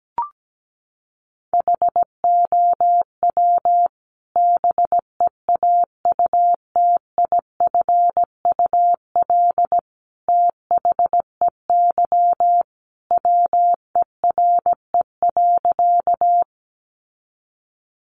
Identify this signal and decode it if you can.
Morse code, heard as a keyed tone, about 17 wpm, 700 Hz. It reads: HOW BEAUTIFUL THEY WERE.
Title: Sentences from Top 900 Words 17wpm